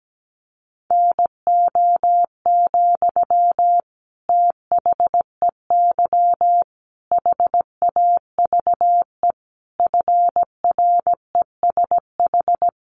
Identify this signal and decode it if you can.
Morse code, heard as a keyed tone, about 17 wpm, 700 Hz. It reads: NO, THEY HAVE FRESH
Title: Sentences from Top 1300 Words 17wpm